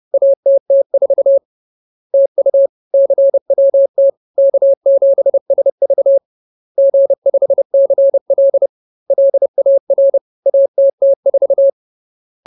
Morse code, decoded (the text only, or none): ATT4 TU CWT K7SV G5CL LAR ATT4